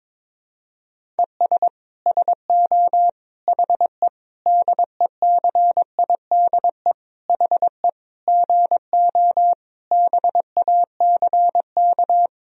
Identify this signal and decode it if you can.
Morse code, heard as a keyed tone, about 22 wpm, 700 Hz. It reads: ES SO HE DECIDE HE GO BACK
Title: Sentences from Top 500 Words and Abbreviations 22wpm